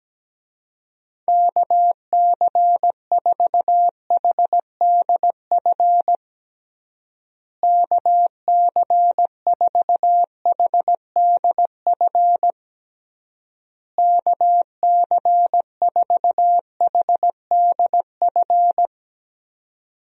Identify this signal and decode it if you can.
Morse code, heard as a keyed tone, about 17 wpm, 700 Hz. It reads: KC4HDF KC4HDF KC4HDF
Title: USA Call Signs - 6 Characters - Repeated 3x - 17wpm